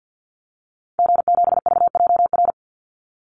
Morse code, heard as a keyed tone, about 50 wpm, 700 Hz. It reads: G7VJR
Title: Call Signs Repeated 2x - 50wpm